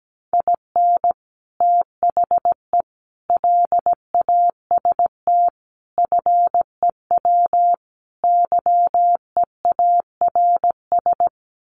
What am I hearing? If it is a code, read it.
Morse code, heard as a keyed tone, about 17 wpm, 700 Hz. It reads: IN THE LAST FEW YEARS
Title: Sets of 5 Words 17wpm